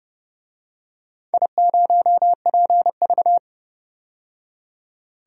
Morse code, heard as a keyed tone, about 29 wpm, 700 Hz. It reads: I0PV
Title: DXCC-100 Call Signs - 4 Characters - 30wpm